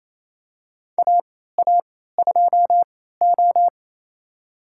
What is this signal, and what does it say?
Morse code, heard as a keyed tone, about 28 wpm, spaced as for 16 wpm, 700 Hz. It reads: AA2O
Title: USA Call Signs - 3x Character Spacing - 28wpm